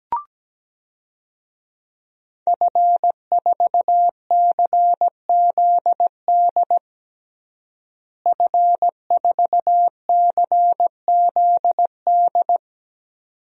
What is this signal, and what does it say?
Morse code, heard as a keyed tone, about 17 wpm, 700 Hz. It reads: F4CZD F4CZD
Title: DXCC-100 Call Signs - 5 Characters - Repeated 2x - 17wpm